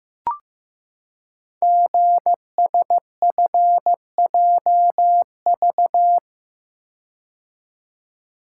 Morse code, heard as a keyed tone, about 15 wpm, 690 Hz. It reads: GSFJV
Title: Code Groups - 5 Letters - 15wpm